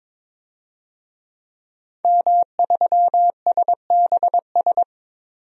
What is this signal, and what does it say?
Morse code, heard as a keyed tone, about 22 wpm, 700 Hz. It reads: M3SBS
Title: DXCC-100 Call Signs - 5 Characters - 22wpm